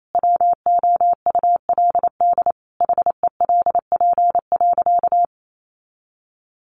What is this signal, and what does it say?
Morse code, heard as a keyed tone, about 28 wpm, 700 Hz. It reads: WOULD HELP.
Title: Sentences from Top 200 Words 28wpm